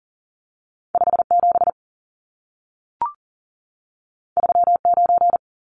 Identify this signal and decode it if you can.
Morse code, heard as a keyed tone, about 40 wpm, 700 Hz. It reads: 57 39
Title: Numbers - 2-Digit 40wpm